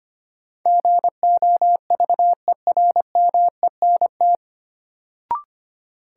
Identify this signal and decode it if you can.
Morse code, heard as a keyed tone, about 25 wpm, 700 Hz. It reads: GOVERMENT
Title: Sentences from Top 1300 Words - New Word Drill - 25wpm